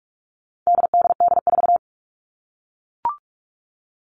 Morse code, heard as a keyed tone, about 44 wpm, 700 Hz. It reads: DDD4
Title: All Cut Numbers - 4-Digit 45wpm